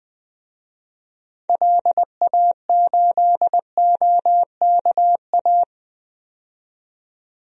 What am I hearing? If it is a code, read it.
Morse code, heard as a keyed tone, about 20 wpm, 700 Hz. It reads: LA8OKA
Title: DXCC-100 Call Signs - 6 Characters - 20wpm